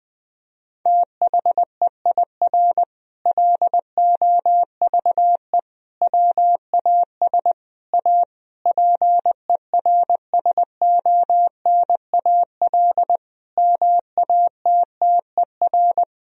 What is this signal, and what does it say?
Morse code, heard as a keyed tone, about 21 wpm, 700 Hz. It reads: THEIR LOVE WAS A PERSONAL MATTER
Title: Sentences from Top 1000 Words - Review Part 4 - 20wpm